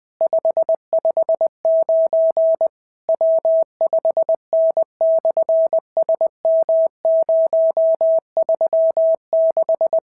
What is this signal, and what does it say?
Morse code, heard as a keyed tone, about 20 wpm, 650 Hz. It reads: 559 W5N/SM036
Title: SOTA QSO Practice 20wpm